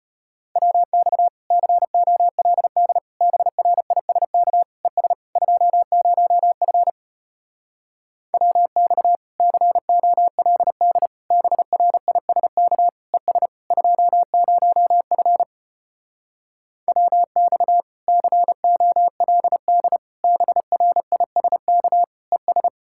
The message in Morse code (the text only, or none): WX COLD BRISK ES 20F WX COLD BRISK ES 20F WX COLD BRISK ES